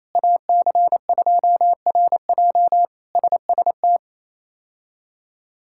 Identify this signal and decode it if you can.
Morse code, heard as a keyed tone, about 28 wpm, 700 Hz. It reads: AC2RJ SST
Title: K1USN SST QSO Practice - 28wpm